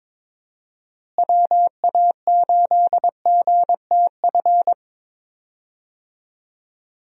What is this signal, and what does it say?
Morse code, heard as a keyed tone, about 22 wpm, 700 Hz. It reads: WA8GTF
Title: USA Call Signs - 6 Characters - Repeated 3x - 22wpm